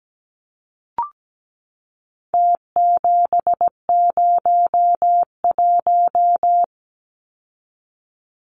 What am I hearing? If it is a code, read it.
Morse code, heard as a keyed tone, about 17 wpm, 700 Hz. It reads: T701
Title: Numbers - Contest Style - 4-Digit 17wpm